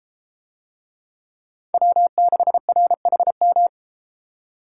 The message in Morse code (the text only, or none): W6RHM